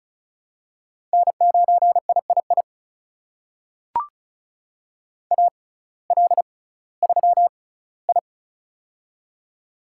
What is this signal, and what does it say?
Morse code, heard as a keyed tone, about 34 wpm, spaced as for 19 wpm, 700 Hz. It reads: N9III AL3I
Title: USA Call Signs - 6x Character Spacing - 35wpm